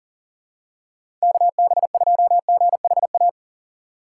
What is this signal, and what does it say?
Morse code, heard as a keyed tone, about 40 wpm, 700 Hz. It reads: KB2GHA
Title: USA Call Signs - 4x Character Spacing - 40wpm